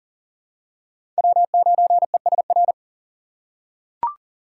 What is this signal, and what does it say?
Morse code, heard as a keyed tone, about 40 wpm, 700 Hz. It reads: W9ESR
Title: USA Call Signs - 6x Character Spacing - 40wpm